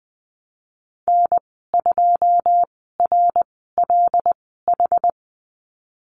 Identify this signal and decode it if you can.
Morse code, heard as a keyed tone, about 20 wpm, spaced as for 15 wpm, 700 Hz. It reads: N2RLH
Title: 088 Lesson 15g - Number 5 - Call Signs 20wpm